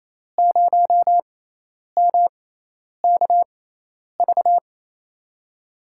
Morse code, heard as a keyed tone, about 28 wpm, spaced as for 10 wpm, 700 Hz. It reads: 0MKV
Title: USA Call Signs - 6 Characters - 6x Character Spacing - 28wpm